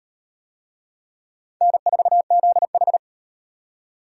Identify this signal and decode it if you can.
Morse code, heard as a keyed tone, about 35 wpm, 700 Hz. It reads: N4ZH